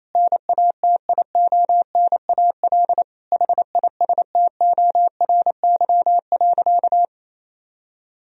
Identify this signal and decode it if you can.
Morse code, heard as a keyed tone, about 28 wpm, 700 Hz. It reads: NATIONAL HISTORY.
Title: Sentences from Top 1000 Words - Review Part 4 - 28wpm